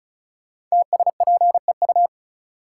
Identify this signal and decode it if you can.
Morse code, heard as a keyed tone, about 36 wpm, 700 Hz. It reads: TSPEU
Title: Code Groups - 5 Letter:Numbers - 35wpm